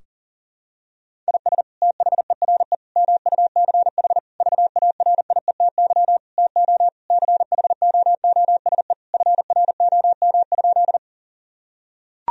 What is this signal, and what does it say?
Morse code, heard as a keyed tone, about 40 wpm, 700 Hz. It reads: IS THERE MUCH VARIETY TO CHOOSE FROM?
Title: Sentences from Top 1000 Words - Review Part 4 - 40wpm